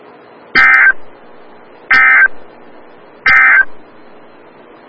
Emergency Alert System (SAME) tones, heard as harsh buzzing bursts. It is heard from 0.4 to 3.7 s.
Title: Des Moines, IA NOAA Weather Radio - WXL57